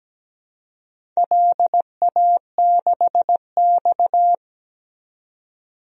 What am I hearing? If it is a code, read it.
Morse code, heard as a keyed tone, about 17 wpm, 700 Hz. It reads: LA6X
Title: DXCC-100 Call Signs - 4 Characters - 17wpm